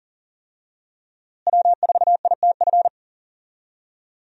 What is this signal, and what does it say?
Morse code, heard as a keyed tone, about 40 wpm, 700 Hz. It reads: W4ITF
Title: USA Call Signs - 5 Characters - Repeated 3x - 40wpm